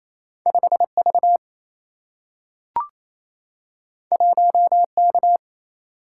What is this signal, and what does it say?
Morse code, heard as a keyed tone, about 28 wpm, 700 Hz. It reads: 5V 1K
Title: Single-Letter Single-Number 28wpm